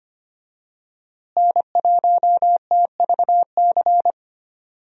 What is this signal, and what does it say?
Morse code, heard as a keyed tone, about 25 wpm, 700 Hz. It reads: N1TVC
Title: USA Call Signs - 5 Characters - Repeated 2x - 25wpm